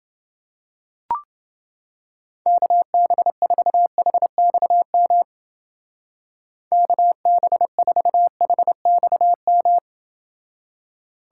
Morse code, heard as a keyed tone, about 29 wpm, 700 Hz. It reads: KB4HXM KB4HXM